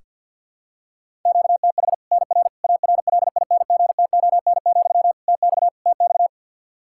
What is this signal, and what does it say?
Morse code, heard as a keyed tone, about 47 wpm, 700 Hz. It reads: QTH NR ARLINGTON, TX TX
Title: K1USN SST QSO Practice 50wpm